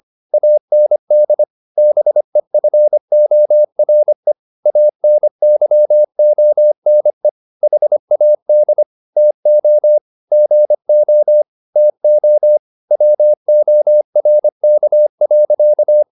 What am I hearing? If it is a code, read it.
Morse code, heard as a keyed tone, about 25 wpm, 600 Hz. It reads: AND BEFORE ANYONE HAD TO GO TO WORK.